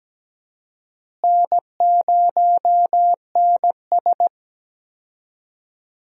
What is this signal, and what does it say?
Morse code, heard as a keyed tone, about 17 wpm, 700 Hz. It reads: N0NS
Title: USA Call Signs - 4 Characters - Repeated 2x - 17wpm